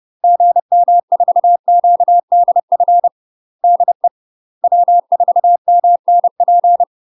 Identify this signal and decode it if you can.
Morse code, heard as a keyed tone, about 30 wpm, 700 Hz. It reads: GM4QDF DE W4MNP
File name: UK MASTER QSOs TRAINING / 30WPM / UK MASTER QSOs 0014